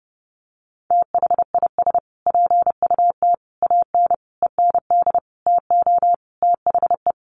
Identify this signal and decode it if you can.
Morse code, heard as a keyed tone, about 30 wpm, 700 Hz. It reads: THIS PUT AN END TO THE
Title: Sentences from Top 1100 Words 30wpm